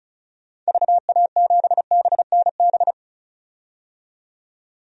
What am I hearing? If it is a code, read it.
Morse code, heard as a keyed tone, about 34 wpm, 700 Hz. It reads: VA7BNB
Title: DXCC-100 Call Signs - 6 Characters - 35wpm